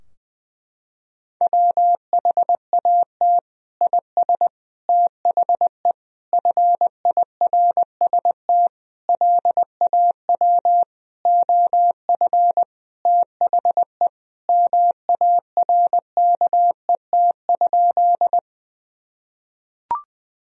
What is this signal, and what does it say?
Morse code, heard as a keyed tone, about 21 wpm, 700 Hz. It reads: WHAT IS THE FIRST LAW OF THE MARKET?
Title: Sentences from Top 1000 Words - Review Part 3 - 20wpm